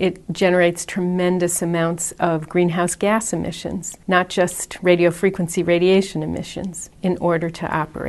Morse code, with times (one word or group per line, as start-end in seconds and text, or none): none